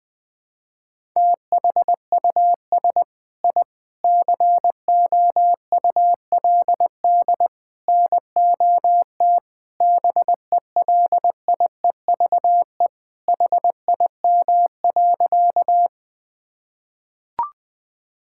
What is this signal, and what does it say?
Morse code, heard as a keyed tone, about 20 wpm, 700 Hz. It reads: THUS I COULD NOT BELIEVE HIM.
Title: Sentences from Top 1000 Words - Review Part 5 - 20wpm